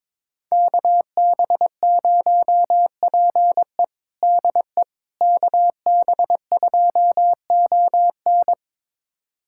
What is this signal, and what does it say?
Morse code, heard as a keyed tone, about 21 wpm, 700 Hz. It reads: KB0PE DE KB2ON
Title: USA Callsign DE Callsign - 5 Characters - 22wpm